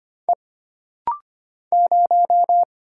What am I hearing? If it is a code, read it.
Morse code, heard as a keyed tone, about 25 wpm, 700 Hz. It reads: E 0
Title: Single Letter-Number - Rapid-Fire 25wpm